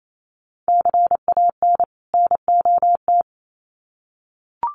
Morse code, heard as a keyed tone, about 28 wpm, 700 Hz. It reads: CAN NOT
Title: Sets of 2 Words 28wpm